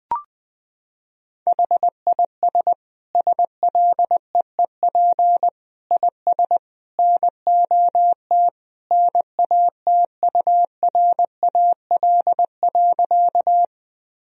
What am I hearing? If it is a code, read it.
Morse code, heard as a keyed tone, about 20 wpm, 700 Hz. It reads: HIS SLEEP IS NOT NATURAL.
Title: Sentences from Top 1000 Words - Review Part 4 - 20wpm